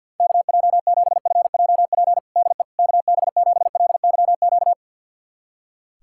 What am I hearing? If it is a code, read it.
Morse code, heard as a keyed tone, about 52 wpm, 700 Hz. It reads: KJ7FJP DE KB7LYQ
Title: USA Callsign DE Callsign - 6 Characters - 50wpm